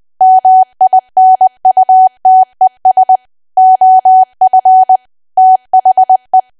Morse code, heard as a keyed tone, about 20 wpm, 750 Hz. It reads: MINUTES OF THE
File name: cwa / W1AW-Level3 / 20WPM